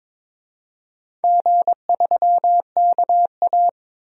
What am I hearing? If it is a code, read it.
Morse code, heard as a keyed tone, about 22 wpm, 700 Hz. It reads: G3KA
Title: DXCC-100 Call Signs - 4 Characters - 22wpm